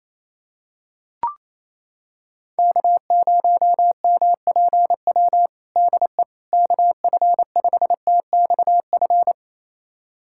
Morse code, heard as a keyed tone, about 27 wpm, 700 Hz. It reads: K0MPW DE KF5TXF
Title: USA Callsign DE Callsign 28wpm